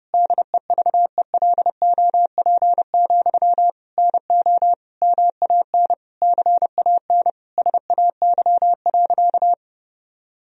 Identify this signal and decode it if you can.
Morse code, heard as a keyed tone, about 30 wpm, 700 Hz. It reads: DEVELOP, NO MAN CAN SAY.
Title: Sentences from Top 1000 Words - Review Part 2 - 30wpm